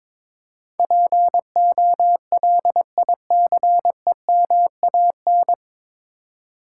Morse code, heard as a keyed tone, about 21 wpm, 700 Hz. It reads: POLICEMAN
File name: 9 Letter Words - First 3 Letters Given - 22wpm